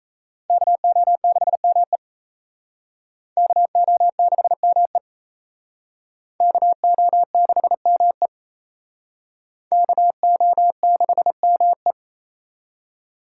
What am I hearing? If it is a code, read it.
Morse code, heard as a keyed tone, about 34 wpm, 700 Hz. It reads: KO6ME KO6ME KO6ME KO6ME
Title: Speed Racing - USA 2x2 Call Signs 28wpm